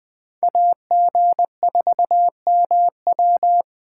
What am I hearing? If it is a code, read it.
Morse code, heard as a keyed tone, about 21 wpm, 700 Hz. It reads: AG4MW